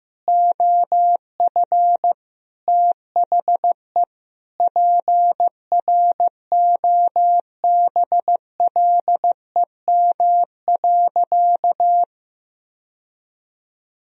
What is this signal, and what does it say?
Morse code, heard as a keyed tone, about 15 wpm, 700 Hz. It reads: OF THE PROBLEM.
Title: Sentences from Top 1000 Words - Review Part 2 - 15wpm